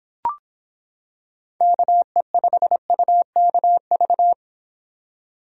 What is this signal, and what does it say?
Morse code, heard as a keyed tone, about 26 wpm, 700 Hz. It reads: KE5UKV